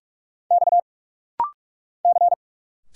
Morse code, heard as a keyed tone, about 46 wpm, 700 Hz. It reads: X C
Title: Single Letters - Rapid-Fire 45wpm